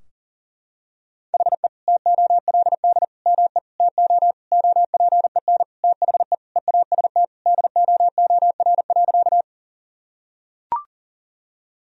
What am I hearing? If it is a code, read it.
Morse code, heard as a keyed tone, about 40 wpm, 700 Hz. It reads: HE TOLD ME TO OPEN THE EAST DOOR.
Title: Sentences from Top 1000 Words - Review Part 3 - 40wpm